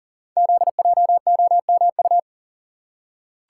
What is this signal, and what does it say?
Morse code, heard as a keyed tone, about 42 wpm, 700 Hz. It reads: ZJOMU
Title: Code Groups - 5 Letters - 40wpm